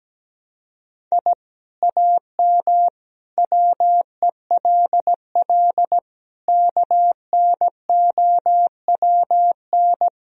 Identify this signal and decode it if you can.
Morse code, heard as a keyed tone, about 17 wpm, 700 Hz. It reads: I AM WELL KNOWN
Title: Sentences from Top 1000 Words - Review Part 3 - 17wpm